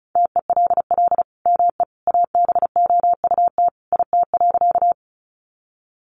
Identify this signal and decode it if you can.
Morse code, heard as a keyed tone, about 34 wpm, 700 Hz. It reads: TELL ME ABOUT IT.
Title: Sentences from Top 200 Words 35wpm